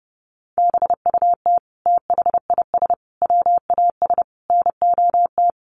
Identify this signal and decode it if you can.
Morse code, heard as a keyed tone, about 28 wpm, 700 Hz. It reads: BUT THIS WAS NOT
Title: Sentences from Top 1500 Words 30wpm